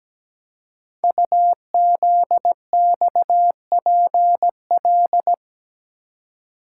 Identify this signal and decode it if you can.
Morse code, heard as a keyed tone, about 17 wpm, 700 Hz. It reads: UZXPL
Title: Code Groups - 5 Letters - 17wpm